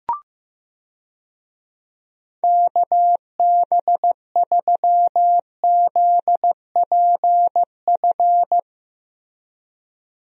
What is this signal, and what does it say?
Morse code, heard as a keyed tone, about 15 wpm, 700 Hz. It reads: KB3ZPF
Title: USA Call Signs - 6 Characters - Repeated 3x - 15wpm